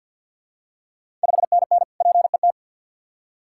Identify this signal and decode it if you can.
Morse code, heard as a keyed tone, about 50 wpm, 690 Hz. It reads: 5NN PET
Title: Ontario QSO Party Exchange 50wpm